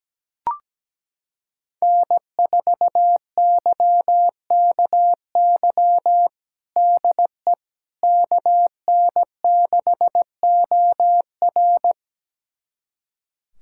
Morse code, heard as a keyed tone, about 17 wpm, 700 Hz. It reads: N4YKY DE KN6OR
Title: USA Callsign DE Callsign 17wpm